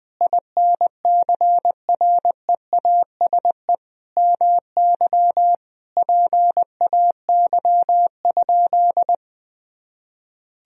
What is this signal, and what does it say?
Morse code, heard as a keyed tone, about 20 wpm, 700 Hz. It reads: INCREASE MY PAY?
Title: Sentences from Top 1000 Words - Review Part 4 - 20wpm